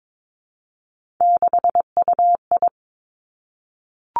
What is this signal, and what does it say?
Morse code, heard as a keyed tone, about 22 wpm, 700 Hz. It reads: 6UI
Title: Three Challenging Letter-Number Combinations 22wpm